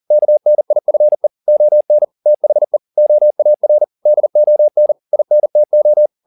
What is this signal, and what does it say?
Morse code, heard as a keyed tone, about 40 wpm, 600 Hz. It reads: KNIFE ON THE OAR DON INTO
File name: the old man and the sea 2 40wpm